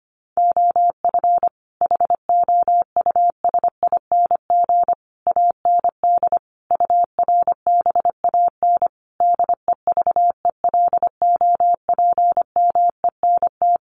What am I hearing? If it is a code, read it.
Morse code, heard as a keyed tone, about 25 wpm, 700 Hz. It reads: OF HOUSING AND URBAN DEVELOPMENT
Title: Sets of 5 Words Encore2 25wpm